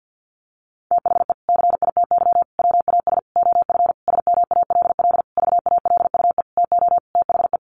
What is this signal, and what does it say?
Morse code, heard as a keyed tone, about 49 wpm, 700 Hz. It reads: THE CITY WAS OF SMALL VALUE TO THE
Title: Sentences from Top 1500 Words 50wpm